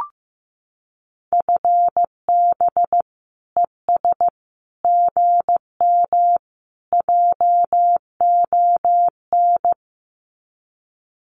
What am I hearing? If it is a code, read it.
Morse code, heard as a keyed tone, about 15 wpm, 700 Hz. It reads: FB ES GM JON
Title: Common QSO Phrases Encore 15wpm